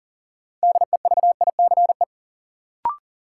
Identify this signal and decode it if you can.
Morse code, heard as a keyed tone, about 38 wpm, 700 Hz. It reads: DEVICE
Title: Sentences from Top 1500 Words - New Word Drill - 40wpm